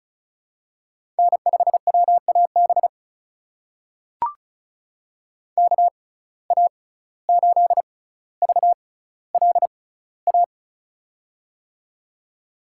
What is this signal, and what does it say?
Morse code, heard as a keyed tone, about 36 wpm, spaced as for 17 wpm, 700 Hz. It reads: N5WAB KA8VLA
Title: USA Call Signs - 6x Character Spacing - 35wpm